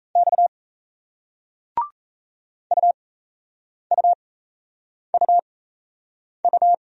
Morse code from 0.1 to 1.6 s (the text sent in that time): X